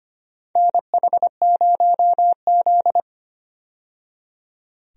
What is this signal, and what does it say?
Morse code, heard as a keyed tone, about 25 wpm, 700 Hz. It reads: NH0Z
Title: USA Call Signs - 4 Characters - 25wpm